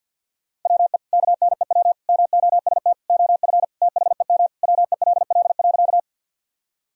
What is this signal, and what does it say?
Morse code, heard as a keyed tone, about 47 wpm, 700 Hz. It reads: WE KNEW MOST OF THEM WELL.
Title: Sentences from Top 1000 Words - Review Part 1 - 50wpm